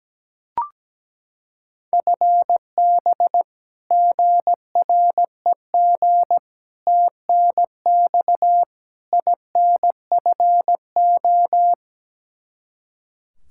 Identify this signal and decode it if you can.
Morse code, heard as a keyed tone, about 17 wpm, 700 Hz. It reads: FB GREG TNX INFO
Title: Practice QSO - Exchange 3 - More Info - 17wpm